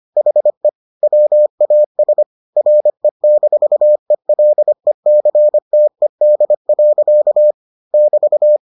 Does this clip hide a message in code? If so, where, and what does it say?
Yes: Morse code from 0.2 to 8.6 s, heard as a keyed tone, about 25 wpm, 600 Hz. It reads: HE WAS RE-ELECTED. <BT>